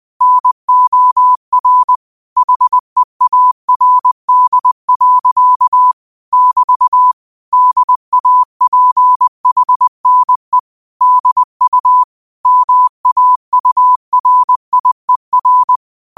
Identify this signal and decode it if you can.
Morse code, heard as a keyed tone, about 20 wpm, 1000 Hz. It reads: NOR HEARD. <BT> DAPHNE DU MAURIER
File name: qotdfiles / 20wpm